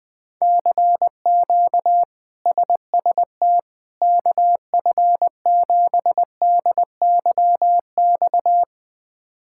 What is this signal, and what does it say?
Morse code, heard as a keyed tone, about 20 wpm, 700 Hz. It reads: CQ SST KF7DYX
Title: K1USN SST QSO Practice - 20wpm